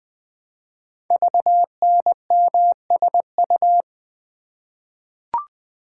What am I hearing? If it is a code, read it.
Morse code, heard as a keyed tone, about 20 wpm, 700 Hz. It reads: VNMSU
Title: Code Groups - 5 Letters - 20wpm